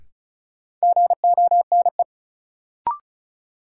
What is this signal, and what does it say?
Morse code, heard as a keyed tone, about 36 wpm, 700 Hz. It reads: GONE
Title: Sentences from Top 1200 Words - New Word Drill Alt - 35wpm